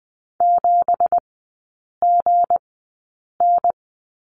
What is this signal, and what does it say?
Morse code, heard as a keyed tone, about 20 wpm, spaced as for 8 wpm, 700 Hz. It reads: 7GN
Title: 218 Lesson 35e - Number 0 - Three Characters 20wpm